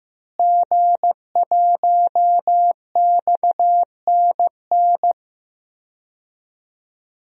Morse code, heard as a keyed tone, about 15 wpm, 700 Hz. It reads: G1XNN
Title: DXCC-100 Call Signs - 5 Characters - Repeated 2x - 15wpm